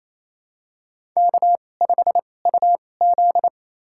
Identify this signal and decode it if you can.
Morse code, heard as a keyed tone, about 27 wpm, spaced as for 20 wpm, 690 Hz. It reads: K5UZ
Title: USA Call Signs - 4 Characters - 2x Character Spacing - 28wpm